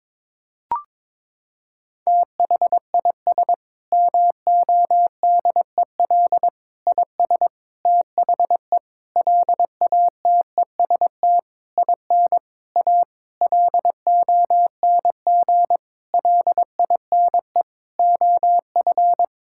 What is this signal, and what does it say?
Morse code, heard as a keyed tone, about 23 wpm, 700 Hz. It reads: THIS MODEL IS THE LATEST IN A LONG LINE OF
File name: Sentences with 7-Word Phrases - 22wpm